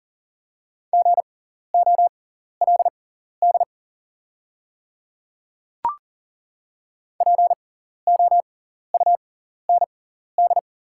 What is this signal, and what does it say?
Morse code, heard as a keyed tone, about 39 wpm, spaced as for 14 wpm, 690 Hz. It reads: GOLD POUND
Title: 6x Character Spacing - Top 500 Words - 40wpm